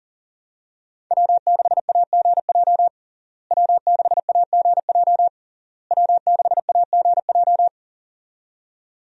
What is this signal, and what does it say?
Morse code, heard as a keyed tone, about 40 wpm, 700 Hz. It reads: W6AGJ W6AGJ W6AGJ
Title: USA Call Signs - 5 Characters - Repeated 3x - 40wpm